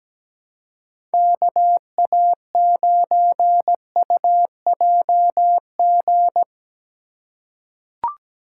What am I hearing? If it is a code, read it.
Morse code, heard as a keyed tone, about 17 wpm, 700 Hz. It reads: K A 9 U J G
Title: USA Call Signs - 6 Characters - 6x Character Spacing - 17wpm